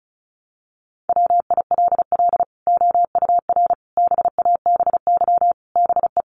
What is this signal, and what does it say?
Morse code, heard as a keyed tone, about 34 wpm, 700 Hz. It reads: WILL OUR BABY BE
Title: Sentences from Top 1300 Words 35wpm